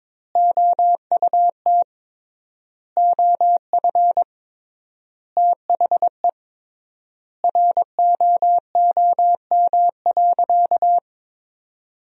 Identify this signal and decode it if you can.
Morse code, heard as a keyed tone, about 22 wpm, 700 Hz. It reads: OUT OF THE ROOM.
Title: 3x Word-Spacing - Sentences from Top 500 Words - 22wpm